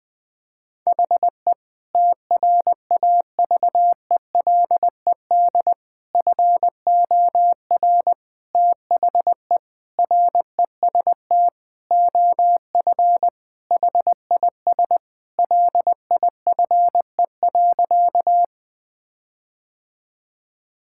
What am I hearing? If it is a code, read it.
Morse code, heard as a keyed tone, about 20 wpm, 700 Hz. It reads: HE TRAVELED FOR THE REST OF HIS LIFE.
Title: Sentences with 5-Word Phrases - 20wpm